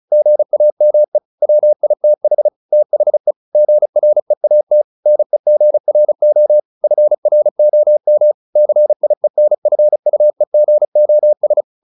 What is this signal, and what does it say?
Morse code, heard as a keyed tone, about 34 wpm, 600 Hz. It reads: GAME WITH THE GREAT NEGRO FROM CIENFUEGOS
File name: the old man and the sea / the old man and the sea 1 35wpm